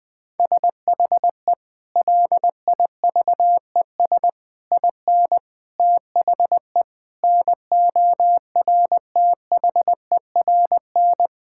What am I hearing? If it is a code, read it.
Morse code, heard as a keyed tone, about 20 wpm, 700 Hz. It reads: SHE LIVES IN THE NORTHERN
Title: Sentences with 6-Word Phrases - Spoken Beginning - 20wpm